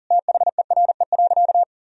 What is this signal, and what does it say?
Morse code, heard as a keyed tone, about 40 wpm, 700 Hz. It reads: THERE.
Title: Sentences from Top 1000 Words - Review Part 2 - 40wpm